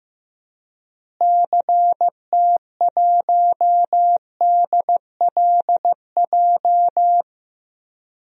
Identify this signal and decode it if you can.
Morse code, heard as a keyed tone, about 15 wpm, 700 Hz. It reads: CT1DLJ
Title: DXCC-100 Call Signs - 6 Characters - Repeated 3x - 15wpm